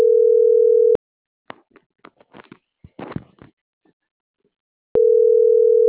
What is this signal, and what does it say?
Ringback tone (North America), 440 + 480 Hz.